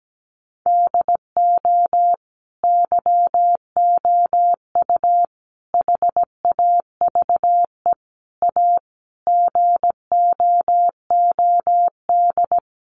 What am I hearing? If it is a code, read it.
Morse code, heard as a keyed tone, about 17 wpm, 700 Hz. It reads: DO YOU HAVE A GOOD
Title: Sentences from Top 1300 Words 17wpm